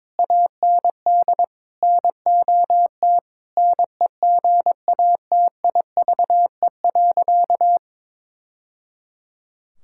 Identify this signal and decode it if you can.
Morse code, heard as a keyed tone, about 22 wpm, 700 Hz. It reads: AND NOT NEGATIVE.
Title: Sentences from Top 1000 Words - Review Part 6 - 22wpm